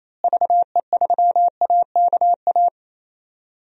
Morse code, heard as a keyed tone, about 28 wpm, 700 Hz. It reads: VE3AKA
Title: DXCC-100 Call Signs - 6 Characters - Repeated 3x - 28wpm